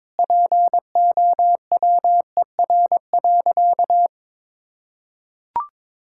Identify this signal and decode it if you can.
Morse code, heard as a keyed tone, about 22 wpm, 700 Hz. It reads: POWER.
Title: Sentences from Top 1000 Words - Review Part 6 - 22wpm